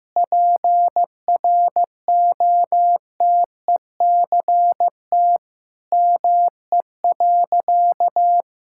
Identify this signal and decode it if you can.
Morse code, heard as a keyed tone, about 16 wpm, 700 Hz. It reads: PROTECT ME.
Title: Sentences from Top 1600 Words 15wpm